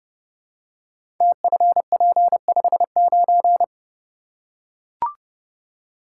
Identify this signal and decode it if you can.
Morse code, heard as a keyed tone, about 30 wpm, 700 Hz. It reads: TFP59
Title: Code Groups - 5 Letter:Numbers - 30wpm